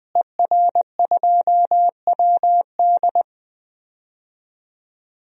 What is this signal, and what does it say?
Morse code, heard as a keyed tone, about 20 wpm, 700 Hz. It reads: ER2WD
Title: DXCC-100 Call Signs - 5 Characters - Repeated 3x - 20wpm